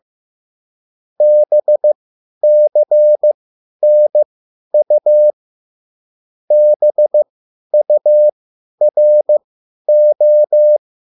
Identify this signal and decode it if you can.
Morse code, heard as a keyed tone, about 15 wpm, spaced as for 10 wpm, 600 Hz. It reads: BCNU BURO